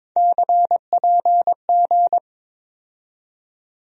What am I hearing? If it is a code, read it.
Morse code, heard as a keyed tone, about 22 wpm, 700 Hz. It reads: CPG
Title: USA Call Sign Postfixes - 3 Letters 22wpm